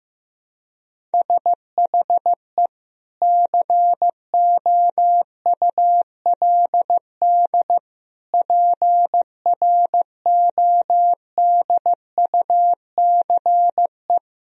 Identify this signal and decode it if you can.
Morse code, heard as a keyed tone, about 15 wpm, 700 Hz. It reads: SHE COULD PRODUCE
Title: Sentences from Top 1000 Words - Review Part 3 - 15wpm